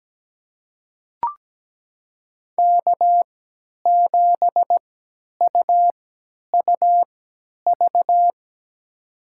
Morse code, heard as a keyed tone, about 17 wpm, spaced as for 10 wpm, 700 Hz. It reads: K7UUV
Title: USA Call Signs - 5 Characters - 3x Character Spacing - 17wpm